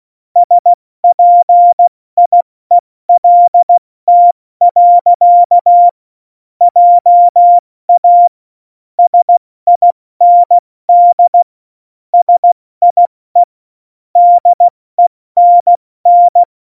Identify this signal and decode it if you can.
Morse code, heard as a keyed tone, about 16 wpm, 700 Hz. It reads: SPIELT. JA SIND SIE DENN
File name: Morsekurs / Witze / Witz 24. November 2020, 70 F80